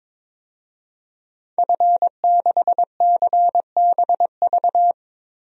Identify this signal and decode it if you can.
Morse code, heard as a keyed tone, about 22 wpm, 700 Hz. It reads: F6CBV
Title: DXCC-100 Call Signs - 5 Characters - 22wpm